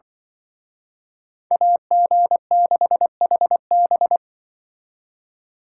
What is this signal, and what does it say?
Morse code, heard as a keyed tone, about 24 wpm, 700 Hz. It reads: AG6HB
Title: Speed Racing - USA 2x2 Call Signs 20wpm